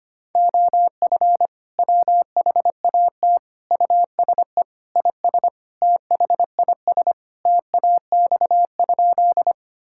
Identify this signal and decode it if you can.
Morse code, heard as a keyed tone, about 25 wpm, 700 Hz. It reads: OF WHAT USE IS THIS TAX?
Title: Sentences from Top 1000 Words - Review Part 5 - 25wpm